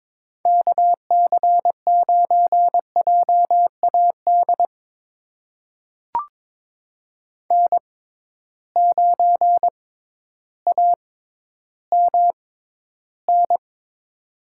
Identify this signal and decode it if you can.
Morse code, heard as a keyed tone, about 22 wpm, spaced as for 12 wpm, 700 Hz. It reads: KC9JAD N9AMN
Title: USA Call Signs - 6x Character Spacing - 22wpm